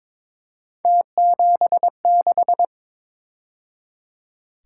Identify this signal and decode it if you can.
Morse code, heard as a keyed tone, about 26 wpm, 700 Hz. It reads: T76
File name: Speed-Racing - Three Letter-Number Combinations - 22wpm